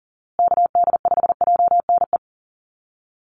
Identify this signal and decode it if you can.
Morse code, heard as a keyed tone, about 40 wpm, 700 Hz. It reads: KD5JNE
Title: USA Call Signs Repeated 3x 40wpm